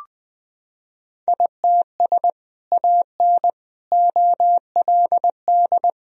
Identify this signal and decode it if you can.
Morse code, heard as a keyed tone, about 20 wpm, 700 Hz. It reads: ITS AN OLD
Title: Sentences from Top 1000 Words - Review Part 2 - 20wpm